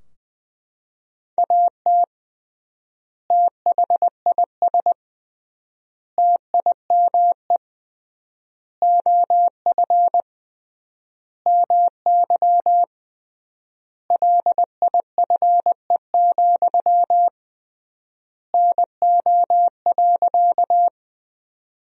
Morse code, heard as a keyed tone, about 20 wpm, 700 Hz. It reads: AT THIS TIME OF MY LIFE, NO.
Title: 3x Word-Spacing - Sentences from Top 300 Words - 20wpm